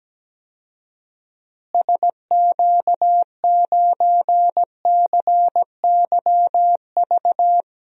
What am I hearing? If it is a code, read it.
Morse code, heard as a keyed tone, about 17 wpm, 700 Hz. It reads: SQ9CYV
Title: DXCC-100 Call Signs - 6 Characters - Repeated 3x - 17wpm